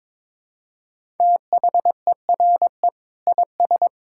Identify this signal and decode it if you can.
Morse code, heard as a keyed tone, about 22 wpm, 700 Hz. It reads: THERE IS
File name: practice / 22wpm / Sentences with 3-Word Phrases - 22wpm